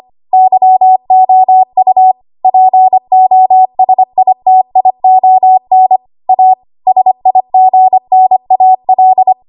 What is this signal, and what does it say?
Morse code, heard as a keyed tone, about 25 wpm, 750 Hz. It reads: YOU POSITION A SIGNAL
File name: Archive / 25 WPM / 25WPM